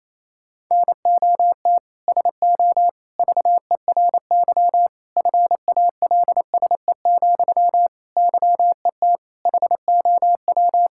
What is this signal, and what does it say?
Morse code, heard as a keyed tone, about 28 wpm, 700 Hz. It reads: NOT SO VERY FALSE, YET HOW
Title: Sentences from Top 1600 Words 28wpm